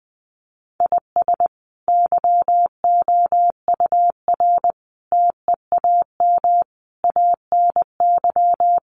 Morse code, heard as a keyed tone, about 20 wpm, 700 Hz. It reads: IS YOUR TEAM ANY
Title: Sentences from Top 1500 Words 20wpm